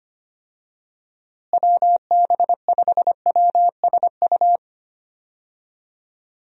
Morse code, heard as a keyed tone, about 25 wpm, 700 Hz. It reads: WB5WSU
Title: USA Call Signs - 6 Characters - Repeated 2x - 25wpm